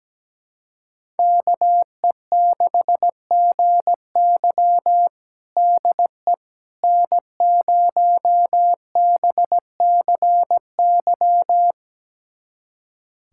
Morse code, heard as a keyed tone, about 17 wpm, 700 Hz. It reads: KE6GY DE N0BCY
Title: USA Callsign DE Callsign 17wpm